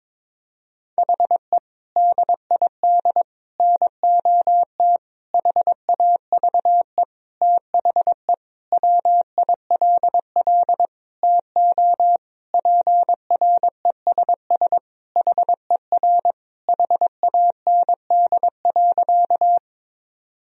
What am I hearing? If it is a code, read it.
Morse code, heard as a keyed tone, about 22 wpm, 700 Hz. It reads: HE DID NOT HAVE THE WILL TO PRESS HER HAND.
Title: Sentences from Top 1000 Words - Review Part 1 - 22wpm